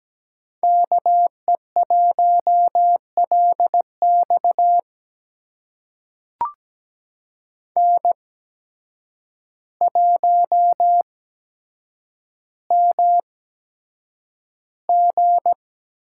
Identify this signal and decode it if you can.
Morse code, heard as a keyed tone, about 17 wpm, spaced as for 8 wpm, 700 Hz. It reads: KE1LX N1MG
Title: USA Call Signs - 5 Characters - 8x Character Spacing - 17wpm